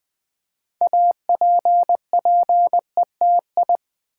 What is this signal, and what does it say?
Morse code, heard as a keyed tone, about 20 wpm, 700 Hz. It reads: APPETI
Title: Build A Word - 3 Letters At A Time - 8-Letter Words - 20wpm